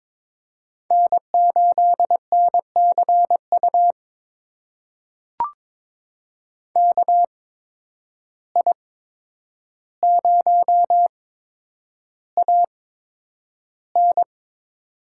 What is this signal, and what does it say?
Morse code, heard as a keyed tone, about 22 wpm, spaced as for 9 wpm, 700 Hz. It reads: N8NCU KI0AN
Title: USA Call Signs - 5 Characters - 8x Character Spacing - 22wpm